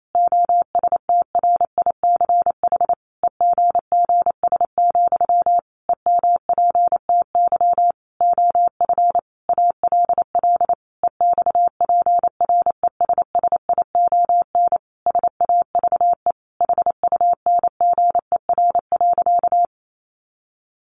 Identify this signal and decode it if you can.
Morse code, heard as a keyed tone, about 28 wpm, 700 Hz. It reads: OSTRICH EGGS, EMPTY OF ALL EXPRESSION SAVE HUNGER.
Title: A Sound Of Thunder 28wpm